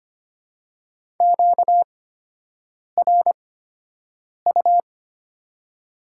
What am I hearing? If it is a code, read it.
Morse code, heard as a keyed tone, about 25 wpm, spaced as for 7 wpm, 700 Hz. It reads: QRU
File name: Top 90 QSO Elements - 8x Character Spacing - 25wpm